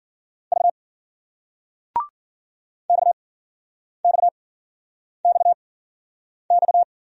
Morse code, heard as a keyed tone, about 54 wpm, spaced as for 9 wpm, 700 Hz. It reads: V XXXX